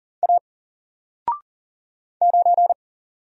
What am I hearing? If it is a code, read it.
Morse code, heard as a keyed tone, about 40 wpm, 700 Hz. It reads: A 9
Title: Single Letter-Number 40wpm